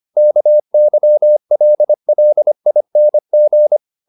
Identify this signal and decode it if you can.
Morse code, heard as a keyed tone, about 25 wpm, 600 Hz. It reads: KYLLING
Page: Ferie 25 wpm